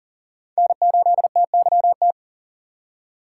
Morse code, heard as a keyed tone, about 40 wpm, 700 Hz. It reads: N8TYT
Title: USA Call Signs - 5 Characters - 4x Character Spacing - 40wpm